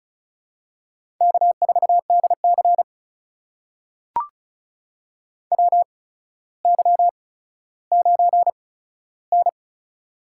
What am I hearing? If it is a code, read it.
Morse code, heard as a keyed tone, about 36 wpm, spaced as for 15 wpm, 700 Hz. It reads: K4DC WY9N
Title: USA Call Signs - 8x Character Spacing - 35wpm